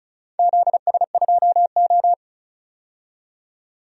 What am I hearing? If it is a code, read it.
Morse code, heard as a keyed tone, about 34 wpm, 700 Hz. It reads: ZS2O
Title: DXCC-100 Call Signs - 4 Characters - 35wpm